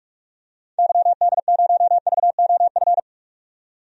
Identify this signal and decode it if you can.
Morse code, heard as a keyed tone, about 43 wpm, 700 Hz. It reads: YD0VOF
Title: DXCC-100 Call Signs - 6 Characters - 45wpm